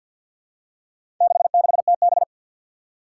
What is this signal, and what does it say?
Morse code, heard as a keyed tone, about 52 wpm, 700 Hz. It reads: 66TB
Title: Four Challenging Letter-Number Combinations 50wpm